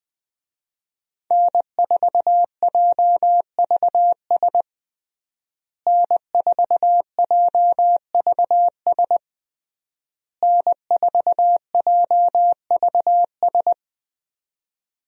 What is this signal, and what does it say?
Morse code, heard as a keyed tone, about 20 wpm, 700 Hz. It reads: N4JVS N4JVS N4JVS
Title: USA Call Signs - 5 Characters - Repeated 3x - 20wpm